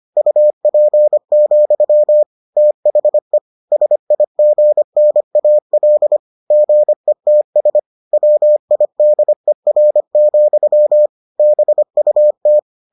Morse code, heard as a keyed tone, about 25 wpm, 600 Hz. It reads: UP, THE SIGNAL GETS WIDER, BUT